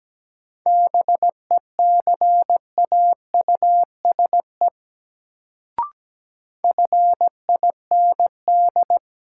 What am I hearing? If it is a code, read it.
Morse code, heard as a keyed tone, about 17 wpm, 700 Hz. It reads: BECAUSE FIND
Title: Sets of 2 Words - New Word Drill - 17wpm